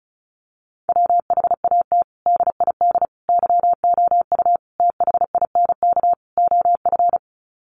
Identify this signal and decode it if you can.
Morse code, heard as a keyed tone, about 34 wpm, 700 Hz. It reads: WHAT DID YOU THINK OF
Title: Sets of 5 Words Encore2 35wpm